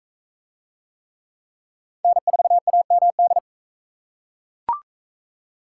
Morse code, heard as a keyed tone, about 38 wpm, 700 Hz. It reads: N4AMD